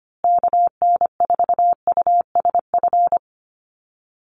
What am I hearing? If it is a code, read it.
Morse code, heard as a keyed tone, about 25 wpm, 700 Hz. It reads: KN4USF
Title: USA Call Signs Repeated 2x 25wpm